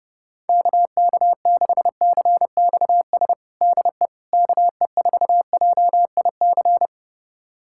Morse code, heard as a keyed tone, about 32 wpm, 700 Hz. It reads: KK6CXS DE KE4JIC
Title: USA Callsign DE Callsign - 6 Characters - 30wpm